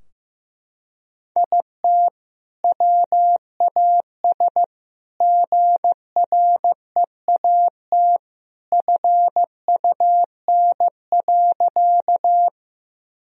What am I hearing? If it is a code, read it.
Morse code, heard as a keyed tone, about 16 wpm, 700 Hz. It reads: IT WAS GREAT FUN.
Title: Sentences from Top 1000 Words - Review Part 4 - 15wpm